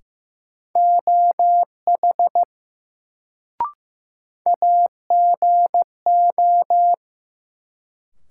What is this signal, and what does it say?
Morse code, heard as a keyed tone, about 15 wpm, 700 Hz. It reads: OH AGO
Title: Sentences from Top 500 Words - New Word Drill - 15wpm